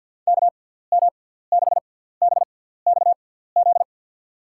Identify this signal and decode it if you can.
Morse code, heard as a keyed tone, about 51 wpm, spaced as for 17 wpm, 700 Hz. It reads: KM6BXZ
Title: USA Call Signs - 6x Character Spacing - 50wpm